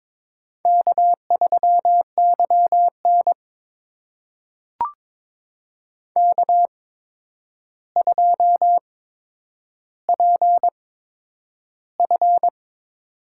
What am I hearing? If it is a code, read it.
Morse code, heard as a keyed tone, about 22 wpm, spaced as for 9 wpm, 700 Hz. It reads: K3YN K2PF
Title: USA Call Signs - 4 Characters - 8x Character Spacing - 22wpm